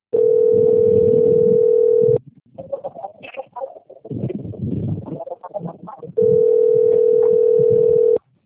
Ringback tone (North America), 440 + 480 Hz.
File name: out- - - - - .14290